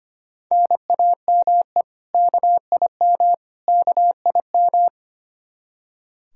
Morse code, heard as a keyed tone, about 25 wpm, 700 Hz. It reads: NAME KIM KIM
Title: K1USN SST QSO Practice 25wpm